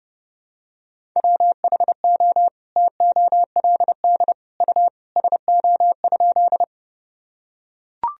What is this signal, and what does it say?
Morse code, heard as a keyed tone, about 30 wpm, 700 Hz. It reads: WHO TOLD U SO?
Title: Sentences from Top 400 Words and Abbreviations 30wpm